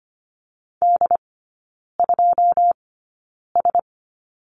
Morse code, heard as a keyed tone, about 25 wpm, spaced as for 9 wpm, 690 Hz. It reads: D2S
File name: Lesson 15e - Number 5 - Three Characters 25wpm